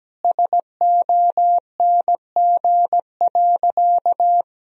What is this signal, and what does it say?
Morse code, heard as a keyed tone, about 17 wpm, 700 Hz. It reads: SONG.
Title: Sentences from Top 1000 Words - Review Part 2 - 17wpm